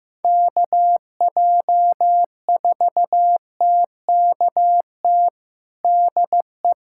Morse code, heard as a keyed tone, about 15 wpm, 700 Hz. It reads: KJ4TKT DE
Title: USA Callsign DE Callsign 15wpm